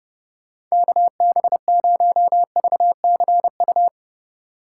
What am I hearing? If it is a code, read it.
Morse code, heard as a keyed tone, about 30 wpm, 700 Hz. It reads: KB0VCU
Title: Speed Racing - USA 2x3 Call Signs 25wpm